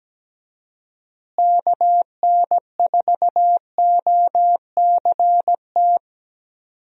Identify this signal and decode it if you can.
Morse code, heard as a keyed tone, about 17 wpm, 700 Hz. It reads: KN4OCT
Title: USA Call Signs - 6 Characters - Repeated 3x - 17wpm